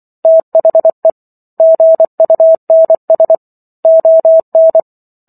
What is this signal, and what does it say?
Morse code, heard as a keyed tone, about 24 wpm, 650 Hz. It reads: THE GUNS ON